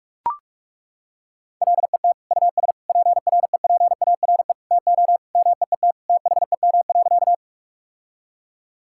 Morse code, heard as a keyed tone, about 42 wpm, 700 Hz. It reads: LET US PREPARE TO MEET THEM.
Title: Sentences from Top 1000 Words - Review Part 6 - 45wpm